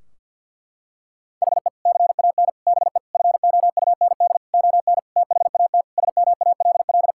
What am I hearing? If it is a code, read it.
Morse code, heard as a keyed tone, about 46 wpm, 700 Hz. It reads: HE CAN BE FOUND ON THAT SMALL
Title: Sentences from Top 1600 Words 50wpm